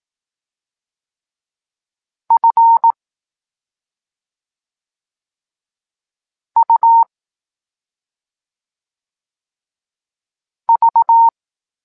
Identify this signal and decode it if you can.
Morse code, heard as a keyed tone, about 18 wpm, 920 Hz. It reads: F U V
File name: Tekst-26 3